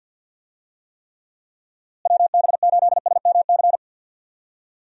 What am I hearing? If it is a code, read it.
Morse code, heard as a keyed tone, about 50 wpm, 700 Hz. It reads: WB8SMC
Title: USA Call Signs - 6 Characters - Repeated 2x - 50wpm